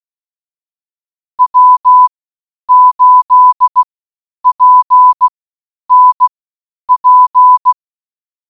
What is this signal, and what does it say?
Morse code, heard as a keyed tone, about 16 wpm, spaced as for 9 wpm, 1000 Hz. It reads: W8PNP